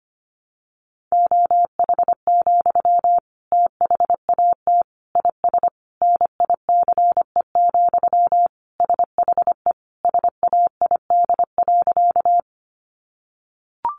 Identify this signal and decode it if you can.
Morse code, heard as a keyed tone, about 25 wpm, 700 Hz. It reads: OH, THAT IS NICE, SHE SAID.
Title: Sentences from Top 1100 Words 25wpm